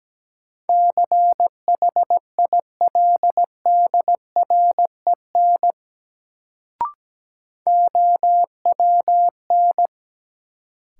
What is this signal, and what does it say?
Morse code, heard as a keyed tone, about 17 wpm, 700 Hz. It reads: CHILDREN OWN
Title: Sentences from Top 300 Words - New Word Drill - 17wpm